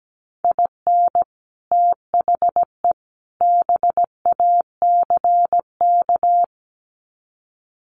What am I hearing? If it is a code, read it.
Morse code, heard as a keyed tone, about 17 wpm, 700 Hz. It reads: IN THE BACK
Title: Sets of 3 Words 17wpm